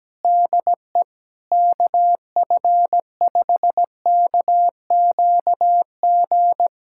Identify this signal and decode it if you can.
Morse code, heard as a keyed tone, about 17 wpm, 700 Hz. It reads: DE KF5KQG
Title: USA Callsign DE Callsign - 6 Characters - 17wpm